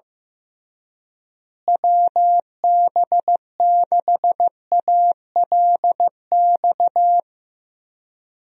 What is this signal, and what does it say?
Morse code, heard as a keyed tone, about 15 wpm, 700 Hz. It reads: WB6ALX
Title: USA Call Signs - 6 Characters - Repeated 3x - 15wpm